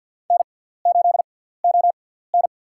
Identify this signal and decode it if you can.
Morse code, heard as a keyed tone, about 49 wpm, spaced as for 17 wpm, 700 Hz. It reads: N8ON
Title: USA Call Signs - 6x Character Spacing - 50wpm